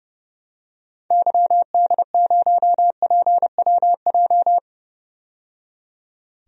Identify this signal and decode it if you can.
Morse code, heard as a keyed tone, about 31 wpm, 700 Hz. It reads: YD0PWJ
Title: DXCC-100 Call Signs - 6 Characters - 30wpm